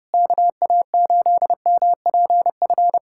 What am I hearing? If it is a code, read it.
Morse code, heard as a keyed tone, about 30 wpm, 700 Hz. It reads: KA8MPF
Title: USA Callsign DE Callsign 30wpm